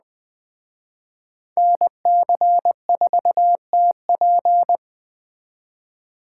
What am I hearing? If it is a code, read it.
Morse code, heard as a keyed tone, about 20 wpm, 700 Hz. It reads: NC4TP